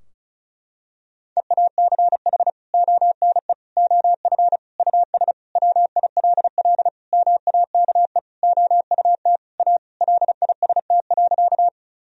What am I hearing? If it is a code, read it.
Morse code, heard as a keyed tone, about 36 wpm, 700 Hz. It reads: EACH ONE OF US WILL MAKE OUT A LIST.
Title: Sentences from Top 1000 Words - Review Part 2 - 35wpm